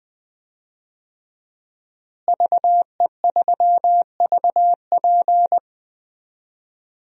Morse code, heard as a keyed tone, about 20 wpm, 700 Hz. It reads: VE3VP
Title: DXCC-100 Call Signs - 5 Characters - Repeated 2x - 20wpm